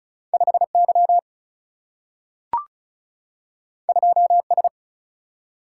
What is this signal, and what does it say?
Morse code, heard as a keyed tone, about 36 wpm, 700 Hz. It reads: 5Y 2S
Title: Single-Letter Single-Number 35wpm